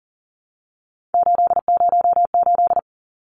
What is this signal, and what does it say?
Morse code, heard as a keyed tone, about 40 wpm, 700 Hz. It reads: 808
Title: Numbers - 3-Digit 40wpm